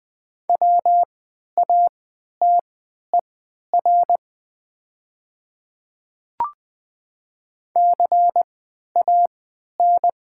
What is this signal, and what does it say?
Morse code, heard as a keyed tone, about 20 wpm, spaced as for 11 wpm, 700 Hz. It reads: WATER CAN
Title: 3x Character Spacing - Top 300 Words - 20wpm